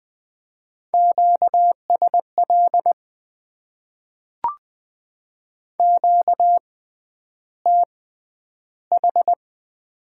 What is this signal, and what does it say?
Morse code, heard as a keyed tone, about 20 wpm, spaced as for 10 wpm, 700 Hz. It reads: QSL QTH
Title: Top 90 QSO Elements - 6x Character Spacing - 20wpm